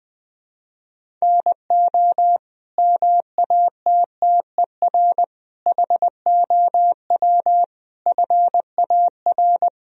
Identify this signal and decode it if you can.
Morse code, heard as a keyed tone, about 21 wpm, 700 Hz. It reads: NO MATTER HOW FAR
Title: Sentences from Top 1000 Words - Review Part 1 - 20wpm